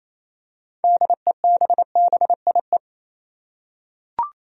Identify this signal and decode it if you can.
Morse code, heard as a keyed tone, about 28 wpm, 700 Hz. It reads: DEBBIE
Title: USA Common Names - 28wpm